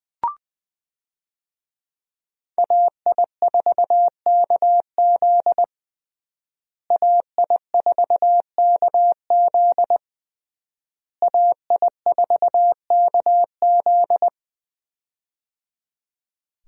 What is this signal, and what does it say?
Morse code, heard as a keyed tone, about 20 wpm, 700 Hz. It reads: AI4KZ AI4KZ AI4KZ
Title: USA Call Signs - 5 Characters - Repeated 3x - 20wpm